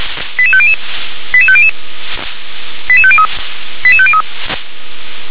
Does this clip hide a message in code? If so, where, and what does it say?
Yes: five-tone selective call (ZVEI) from 0.4 to 4.2 s, heard as quick runs of five tones, tones of about 70 ms. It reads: 88500 88500 88503 88503